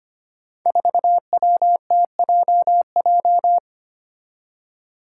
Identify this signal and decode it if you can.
Morse code, heard as a keyed tone, about 25 wpm, 700 Hz. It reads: 4WTJJ
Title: Five Challenging Letter-Number Combinations 25wpm